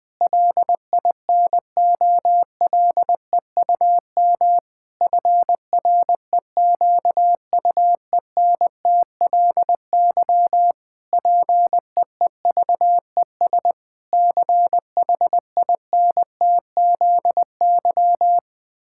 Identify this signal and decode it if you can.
Morse code, heard as a keyed tone, about 21 wpm, 700 Hz. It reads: LINOLEUM FREQUENTLY PEEVES CHINTZY
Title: Pangram Challenge v2 - 20wpm